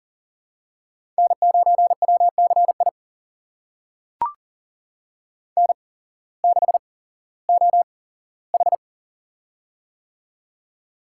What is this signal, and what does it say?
Morse code, heard as a keyed tone, about 40 wpm, spaced as for 19 wpm, 700 Hz. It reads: N9WCI N6OH
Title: USA Call Signs - 8x Character Spacing - 40wpm